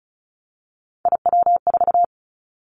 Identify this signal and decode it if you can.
Morse code, heard as a keyed tone, about 36 wpm, 690 Hz. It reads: IW4
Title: Callsign Prefixes for Top 50 DXCC - 35wpm